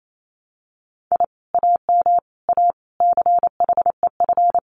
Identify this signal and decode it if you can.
Morse code, heard as a keyed tone, about 28 wpm, 700 Hz. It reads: I AM A CHEF
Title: Common QSO Phrases Encore 28wpm